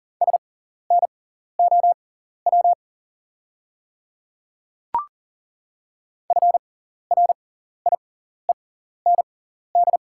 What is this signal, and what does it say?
Morse code, heard as a keyed tone, about 40 wpm, spaced as for 14 wpm, 690 Hz. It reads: SNOW FRIEND
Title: 6x Character Spacing - Top 500 Words - 40wpm